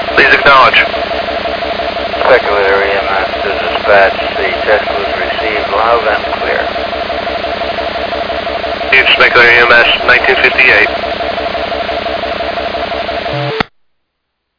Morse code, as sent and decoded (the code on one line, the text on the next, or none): none